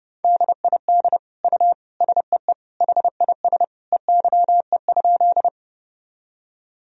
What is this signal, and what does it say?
Morse code, heard as a keyed tone, about 29 wpm, 700 Hz. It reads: DID U SEE HIS EYE?
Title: Sentences from Top 300 Words and Abbreviations 30wpm